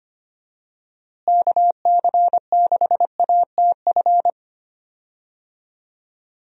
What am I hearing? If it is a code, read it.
Morse code, heard as a keyed tone, about 25 wpm, 700 Hz. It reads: KC6ATF
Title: USA Call Signs - 6 Characters - Repeated 3x - 25wpm